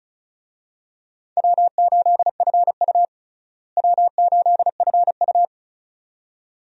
Morse code, heard as a keyed tone, about 36 wpm, 700 Hz. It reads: W8FU W8FU
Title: USA Call Signs - 4 Characters - Repeated 2x - 35wpm